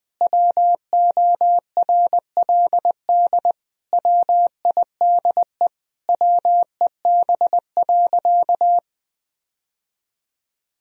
Morse code, heard as a keyed tone, about 20 wpm, 700 Hz. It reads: WORLD WIDE WEB.
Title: Sentences from Top 1000 Words - Review Part 4 - 20wpm